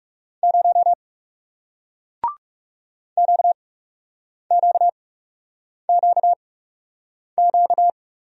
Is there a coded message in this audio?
Yes: Morse code from 0.4 to 7.9 s, heard as a keyed tone, about 39 wpm, spaced as for 8 wpm, 700 Hz. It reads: 0 QQQQ